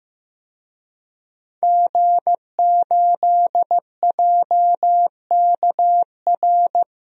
Morse code, heard as a keyed tone, about 15 wpm, 700 Hz. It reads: G8JKR
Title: DXCC-100 Call Signs - 5 Characters - 15wpm